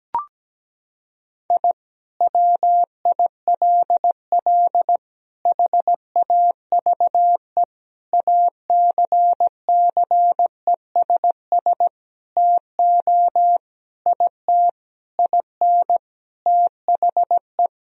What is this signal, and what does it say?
Morse code, heard as a keyed tone, about 17 wpm, 700 Hz. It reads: I WILL HAVE ACCESS TO IT IN THE
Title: Sentences from Top 1000 Words - Review Part 4 - 17wpm